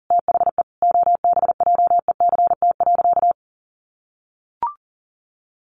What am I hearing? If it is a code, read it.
Morse code, heard as a keyed tone, about 40 wpm, 700 Hz. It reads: THE OBJECT.
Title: Sentences from Top 800 Words 40wpm